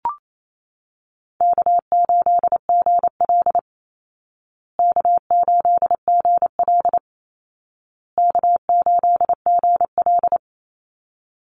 Morse code, heard as a keyed tone, about 28 wpm, 700 Hz. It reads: K8GL K8GL K8GL
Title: Easy Call Signs Repeated 3x - 28wpm